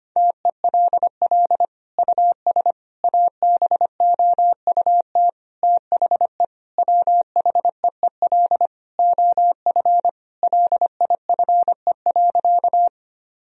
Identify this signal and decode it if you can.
Morse code, heard as a keyed tone, about 25 wpm, 700 Hz. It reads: TELL US ABOUT THE WHEEL OF LIFE.
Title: Sentences from Top 1000 Words - Review Part 3 - 25wpm